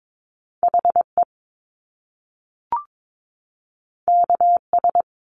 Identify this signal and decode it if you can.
Morse code, heard as a keyed tone, about 22 wpm, 700 Hz. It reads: HE KS
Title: Top Two Letter Combinations in Words 22wpm